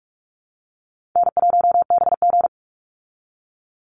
Morse code, heard as a keyed tone, about 44 wpm, 700 Hz. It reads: N1BG
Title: Call Signs Repeated 2x - 45wpm